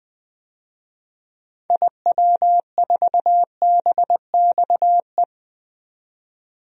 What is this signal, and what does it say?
Morse code, heard as a keyed tone, about 20 wpm, 700 Hz. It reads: IW4BXE
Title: DXCC-100 Call Signs - 6 Characters - 20wpm